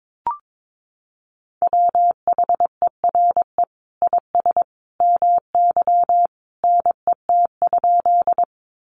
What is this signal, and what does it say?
Morse code, heard as a keyed tone, about 22 wpm, 700 Hz. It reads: WHERE IS MY NET?
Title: Sentences from Top 1400 Words 22wpm